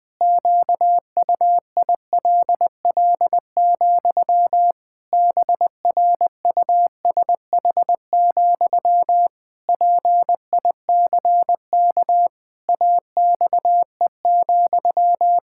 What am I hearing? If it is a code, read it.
Morse code, heard as a keyed tone, about 20 wpm, 700 Hz. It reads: QUILL, BRUSH, PICK AXE,
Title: Pangram Challenge v2 - 20wpm